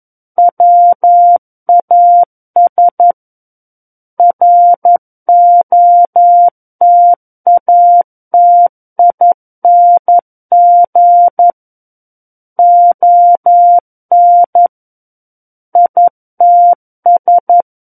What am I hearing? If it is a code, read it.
Morse code, heard as a keyed tone, about 11 wpm, 700 Hz. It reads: WAS ROTATING ON ITS
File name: 11 WPM